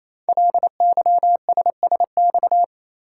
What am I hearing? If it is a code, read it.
Morse code, heard as a keyed tone, about 27 wpm, 700 Hz. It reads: LYSSX
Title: Code Groups - 5 Letters - 28wpm